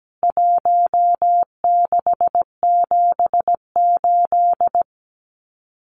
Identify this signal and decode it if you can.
Morse code, heard as a keyed tone, about 17 wpm, 700 Hz. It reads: 1678
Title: CWT Exchange 17wpm